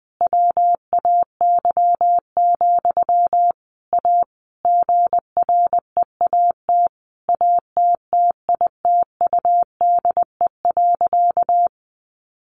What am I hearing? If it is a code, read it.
Morse code, heard as a keyed tone, about 20 wpm, 700 Hz. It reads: WAY, A GREAT ATTITUDE.
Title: Sentences from Top 1300 Words 20wpm